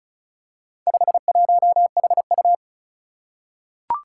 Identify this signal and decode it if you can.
Morse code, heard as a keyed tone, about 36 wpm, 700 Hz. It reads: 51HU
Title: Four Challenging Letter-Number Combinations 35wpm